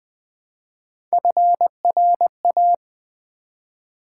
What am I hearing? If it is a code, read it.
Morse code, heard as a keyed tone, about 20 wpm, 700 Hz. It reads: FRA
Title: Build A Word - 1 Letter At A Time - 3 to 5-Letter Words - 20wpm